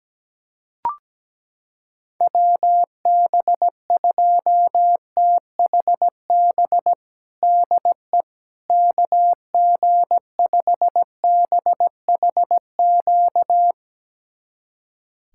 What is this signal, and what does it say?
Morse code, heard as a keyed tone, about 17 wpm, 700 Hz. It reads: WB2THB DE KG5BHQ
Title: USA Callsign DE Callsign 17wpm